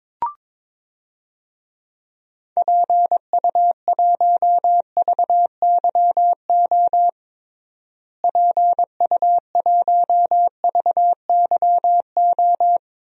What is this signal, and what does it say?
Morse code, heard as a keyed tone, about 21 wpm, 700 Hz. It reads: PU1VYO PU1VYO
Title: DXCC-100 Call Signs - 6 Characters - Repeated 2x - 22wpm